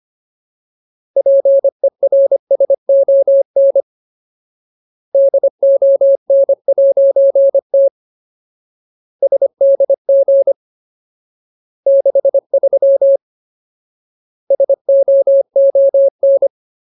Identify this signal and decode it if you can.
Morse code, heard as a keyed tone, about 25 wpm, 550 Hz. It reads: PERSON DON'T SDG 63 SOON
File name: Ultimate Word List / 25WPM Ultimate